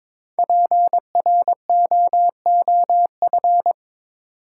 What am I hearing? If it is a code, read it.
Morse code, heard as a keyed tone, about 22 wpm, 690 Hz. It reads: PROOF
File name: 5 Letter Words - First 3 Letters Given - 22wpm